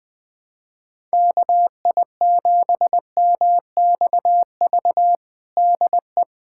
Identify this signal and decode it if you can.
Morse code, heard as a keyed tone, about 21 wpm, 700 Hz. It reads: KI7MXV DE
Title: USA Callsign DE Callsign 20wpm